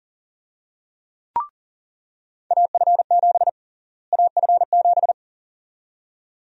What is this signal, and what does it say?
Morse code, heard as a keyed tone, about 40 wpm, 700 Hz. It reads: AF7 AF7
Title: USA Call Sign Prefixes - Repeated 2x - 40wpm